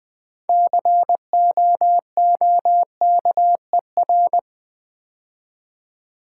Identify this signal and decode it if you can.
Morse code, heard as a keyed tone, about 20 wpm, 700 Hz. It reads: COOKER
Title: Build A Word - 3 Letters At A Time - 6-Letter Words - 20wpm